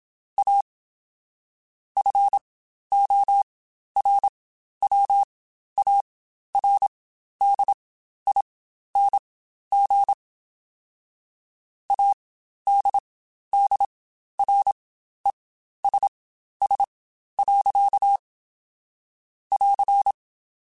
Morse code, heard as a keyed tone, about 27 wpm, spaced as for 12 wpm, 800 Hz. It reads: A FORWARDING ADDRESS. <AR>
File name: Spaced / Chuckles Spaced-mp3s / Temptation sp @25